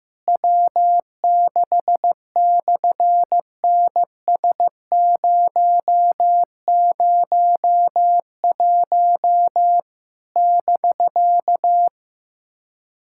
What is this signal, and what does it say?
Morse code, heard as a keyed tone, about 15 wpm, 700 Hz. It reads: W6/NS001 <BK>
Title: SOTA QSO Practice 15wpm